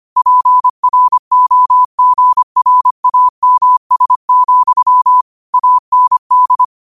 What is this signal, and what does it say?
Morse code, heard as a keyed tone, about 25 wpm, 1000 Hz. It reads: PROGRAMS, AND
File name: qotdfiles / 25wpm